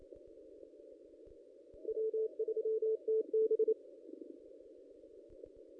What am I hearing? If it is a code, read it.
Morse code, heard as a keyed tone, about 28 wpm, 440 Hz. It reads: W3TB
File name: contests / ss-cw-k5tr-op-n5rz -1800